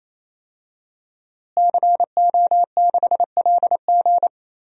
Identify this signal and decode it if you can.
Morse code, heard as a keyed tone, about 28 wpm, 700 Hz. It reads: CO6LG
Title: DXCC-100 Call Signs - 5 Characters - Repeated 3x - 28wpm